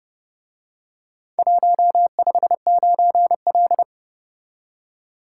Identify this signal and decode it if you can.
Morse code, heard as a keyed tone, about 29 wpm, 700 Hz. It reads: 159L